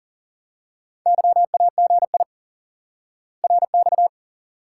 Morse code, heard as a keyed tone, about 40 wpm, 700 Hz. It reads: YAGI RX
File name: IQE Course 3 - Lesson 89 - YAGI - Two Elements - 40wpm